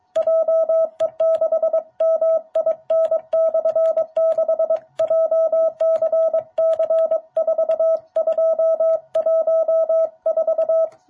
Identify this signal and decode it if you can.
Morse code, heard as a keyed tone, about 23 wpm, 660 Hz. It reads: JE6MIN/6 JCC4214
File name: jh8cbh / 2021cw